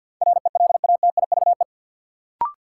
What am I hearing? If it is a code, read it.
Morse code, heard as a keyed tone, about 53 wpm, 700 Hz. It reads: RELATIVE
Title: Sentences from Top 1400 Words - New Word Drill - 50wpm